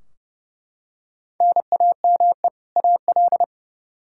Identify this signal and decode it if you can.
Morse code, heard as a keyed tone, about 30 wpm, 700 Hz. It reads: NAME AL
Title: K1USN SST QSO Practice 30wpm